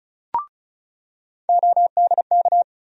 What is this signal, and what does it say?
Morse code, heard as a keyed tone, about 36 wpm, 700 Hz. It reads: ODK
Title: USA Call Sign Postfixes - 3 Letters 35wpm